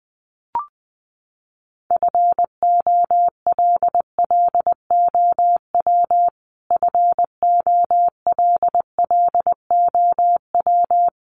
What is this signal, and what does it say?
Morse code, heard as a keyed tone, about 20 wpm, 700 Hz. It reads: FOLLOW FOLLOW
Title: Top 200 Words Repeated 20wpm